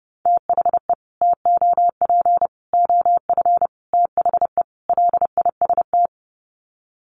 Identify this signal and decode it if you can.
Morse code, heard as a keyed tone, about 30 wpm, 700 Hz. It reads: THE TOP OF THE LIST
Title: Sets of 5 Words Encore2 30wpm